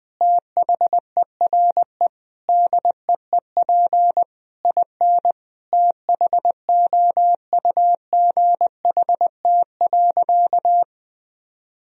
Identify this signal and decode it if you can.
Morse code, heard as a keyed tone, about 20 wpm, 700 Hz. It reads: THERE DEEP IN THOUGHT.
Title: Sentences from Top 1000 Words - Review Part 3 - 20wpm